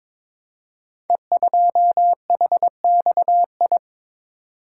Morse code, heard as a keyed tone, about 22 wpm, 700 Hz. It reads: E2HXI
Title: Code Groups - 5 Letter:Numbers - 22wpm